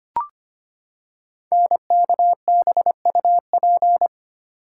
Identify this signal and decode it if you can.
Morse code, heard as a keyed tone, about 25 wpm, 700 Hz. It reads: NKBUP
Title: Code Groups - 5 Letters - 25wpm